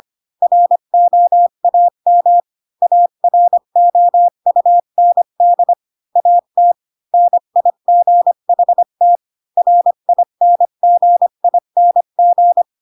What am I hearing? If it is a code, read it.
Morse code, heard as a keyed tone, about 25 wpm, 700 Hz. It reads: ROAM AROUND AT NIGHT RINGING